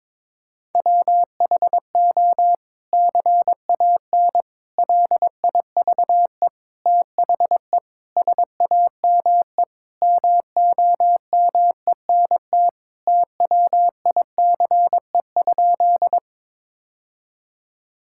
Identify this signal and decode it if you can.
Morse code, heard as a keyed tone, about 22 wpm, 700 Hz. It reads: WHO CAN LIVE THE SAME MOMENT TWICE?
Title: Sentences from Top 1600 Words 22wpm